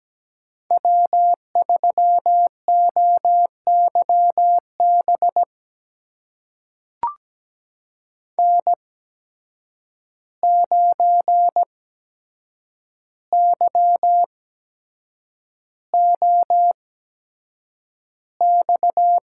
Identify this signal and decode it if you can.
Morse code, heard as a keyed tone, about 17 wpm, spaced as for 7 wpm, 700 Hz. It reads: W3OYB N9YOX
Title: USA Call Signs - 5 Characters - 8x Character Spacing - 17wpm